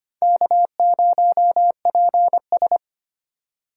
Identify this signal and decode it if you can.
Morse code, heard as a keyed tone, about 32 wpm, 700 Hz. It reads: K0PS